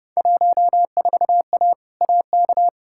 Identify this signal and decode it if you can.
Morse code, heard as a keyed tone, about 30 wpm, 700 Hz. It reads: 14A AK
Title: Field Day Exchanges Encore - 30wpm